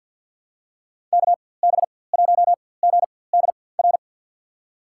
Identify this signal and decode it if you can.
Morse code, heard as a keyed tone, about 52 wpm, spaced as for 23 wpm, 700 Hz. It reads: KB1GDR
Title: USA Call Signs - 6 Characters - 4x Character Spacing - 50wpm